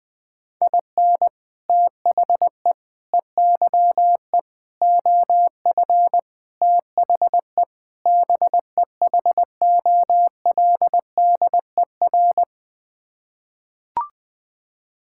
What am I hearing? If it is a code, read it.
Morse code, heard as a keyed tone, about 21 wpm, 700 Hz. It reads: IN THE EYE OF THE BEHOLDER
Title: Sets of 6 Words - 20wpm